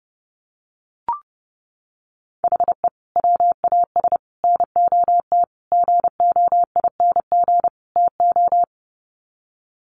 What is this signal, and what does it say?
Morse code, heard as a keyed tone, about 30 wpm, 700 Hz. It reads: HE WAS NOT GOING TO
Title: Sets of 5 Words 30wpm